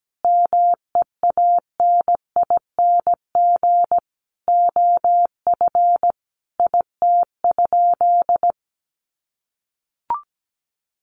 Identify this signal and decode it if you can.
Morse code, heard as a keyed tone, about 17 wpm, 700 Hz. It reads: MEANING OF IT?
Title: Sentences from Top 900 Words 17wpm